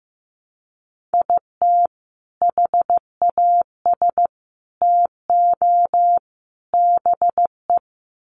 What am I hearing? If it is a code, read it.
Morse code, heard as a keyed tone, about 15 wpm, 700 Hz. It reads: IT HAS TO BE
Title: Sets of 4 Words 15wpm